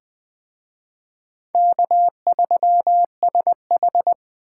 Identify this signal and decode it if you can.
Morse code, heard as a keyed tone, about 20 wpm, 700 Hz. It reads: K3SH
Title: Speed Racing - USA 1x2 Call Signs 17wpm